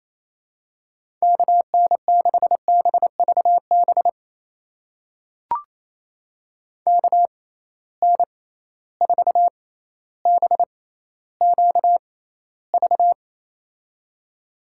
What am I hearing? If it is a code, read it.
Morse code, heard as a keyed tone, about 28 wpm, spaced as for 14 wpm, 700 Hz. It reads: KN6BVB KN4BQV
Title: USA Call Signs - 6x Character Spacing - 28wpm